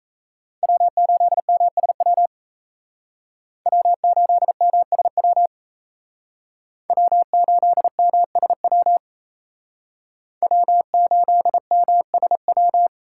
Morse code, heard as a keyed tone, about 35 wpm, 700 Hz. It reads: W8MSW W8MSW W8MSW W8MSW
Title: Speed Racing - USA Call Signs 28wpm